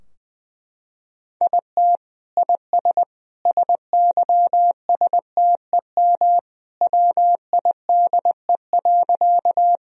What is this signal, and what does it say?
Morse code, heard as a keyed tone, about 20 wpm, 700 Hz. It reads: IT IS SYSTEM WIDE.
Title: Sentences from Top 1000 Words - Review Part 3 - 20wpm